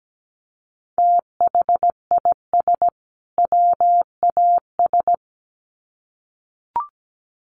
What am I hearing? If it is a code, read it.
Morse code, heard as a keyed tone, about 17 wpm, 700 Hz. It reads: THIS WAS
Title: Sets of 2 Words 17wpm